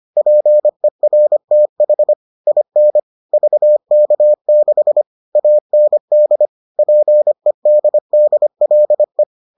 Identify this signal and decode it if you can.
Morse code, heard as a keyed tone, about 25 wpm, 600 Hz. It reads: PERTH IN VK6 AND PEDDLE